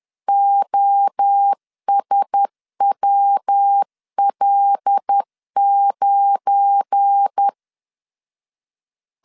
Morse code, heard as a keyed tone, about 10 wpm, 800 Hz. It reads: OSWL9
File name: LEZIONI / 7 GF9Z / 7 GF9 10wpm 2747